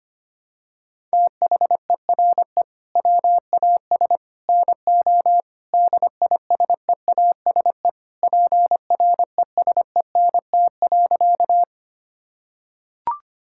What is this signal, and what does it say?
Morse code, heard as a keyed tone, about 25 wpm, 700 Hz. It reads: THERE WAS NO DISEASE PRESENT.
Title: Sentences from Top 1000 Words - Review Part 6 - 25wpm